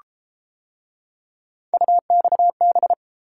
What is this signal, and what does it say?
Morse code, heard as a keyed tone, about 31 wpm, 700 Hz. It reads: UXB